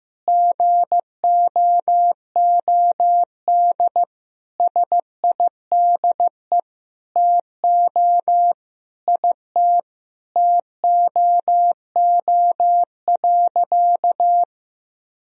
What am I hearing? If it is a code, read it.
Morse code, heard as a keyed tone, about 15 wpm, 700 Hz. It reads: GOOD SIDE TO IT TOO.
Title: Sentences from Top 1000 Words - Review Part 1 - 15wpm